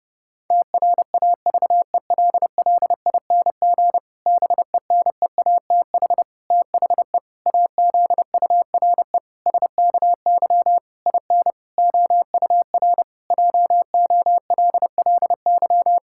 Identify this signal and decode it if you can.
Morse code, heard as a keyed tone, about 30 wpm, 700 Hz. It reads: TRAVELLING BENEATH THE AZURE SKY IN OUR JOLLY
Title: Pangram Challenge v2 - 30wpm